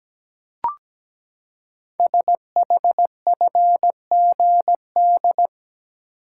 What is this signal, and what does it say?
Morse code, heard as a keyed tone, about 17 wpm, 700 Hz. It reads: SHFGD
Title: Code Groups - 5 Letters - 17wpm